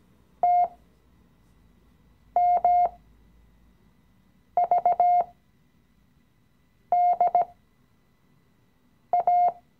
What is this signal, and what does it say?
Morse code, heard as a keyed tone, about 17 wpm, spaced as for 5 wpm, 710 Hz. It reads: TMVDA
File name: CW-Online-Kurs / Anfänger / Lektion 017